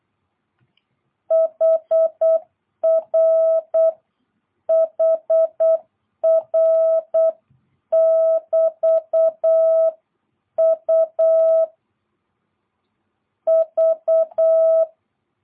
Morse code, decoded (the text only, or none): HR HR <BT> U V